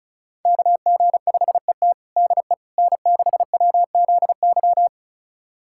Morse code, heard as a keyed tone, about 36 wpm, 700 Hz. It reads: KG5ET DE N6WZY
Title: USA Callsign DE Callsign - 5 Characters - 35wpm